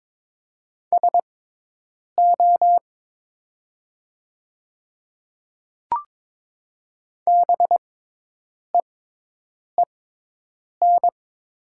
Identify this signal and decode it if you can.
Morse code, heard as a keyed tone, about 22 wpm, spaced as for 8 wpm, 700 Hz. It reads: SO BEEN
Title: 6x Character Spacing - Top 100 Words - 22wpm